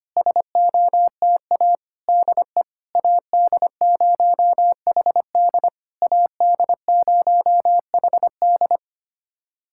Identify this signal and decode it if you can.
Morse code, heard as a keyed tone, about 25 wpm, 700 Hz. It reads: SOTA DE AD0HD AD0HD
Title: SOTA QSO Practice 25wpm